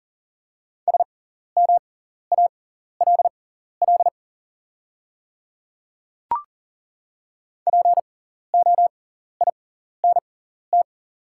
Morse code, heard as a keyed tone, about 39 wpm, spaced as for 14 wpm, 690 Hz. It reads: SMALL POINT
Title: 6x Character Spacing - Top 200 Words - 40wpm